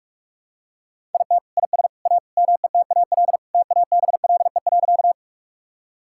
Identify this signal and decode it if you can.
Morse code, heard as a keyed tone, about 45 wpm, 700 Hz. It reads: IT IS A METAL TABLE.
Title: Sentences from Top 1000 Words - Review Part 6 - 45wpm